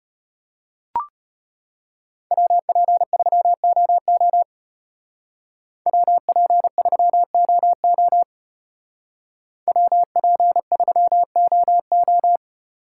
Morse code, heard as a keyed tone, about 32 wpm, 700 Hz. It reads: WP3OO WP3OO WP3OO